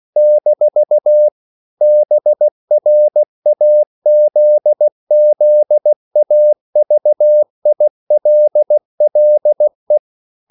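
Morse code, heard as a keyed tone, about 16 wpm, 600 Hz. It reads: - BRAZZAVILLE
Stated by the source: CW practice of national capitals at 16 words per minute, from ON6ZQ.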